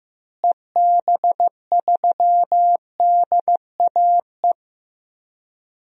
Morse code, heard as a keyed tone, about 15 wpm, 700 Hz. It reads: EB3DAE
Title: DXCC-100 Call Signs - 6 Characters - Repeated 3x - 15wpm